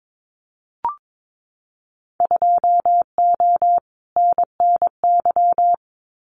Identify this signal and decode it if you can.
Morse code, heard as a keyed tone, about 22 wpm, 700 Hz. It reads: 2O NNY
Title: Winter Field Day Exchanges 22wpm